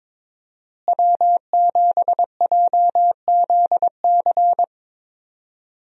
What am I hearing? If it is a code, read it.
Morse code, heard as a keyed tone, about 22 wpm, 700 Hz. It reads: W7JZC
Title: USA Call Signs - 5 Characters - Repeated 2x - 22wpm